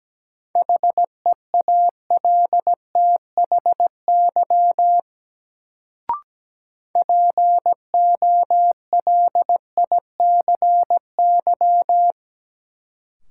Morse code, heard as a keyed tone, about 17 wpm, 700 Hz. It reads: HEALTHY POLICY
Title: Sentences from Top 1000 Words - Review Part 5 - New Word Drill - 17wpm